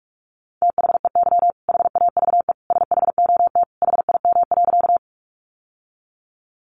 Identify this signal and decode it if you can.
Morse code, heard as a keyed tone, about 44 wpm, 700 Hz. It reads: THEY HAVE SHOT HIM.
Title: Sentences from Top 1100 Words 45wpm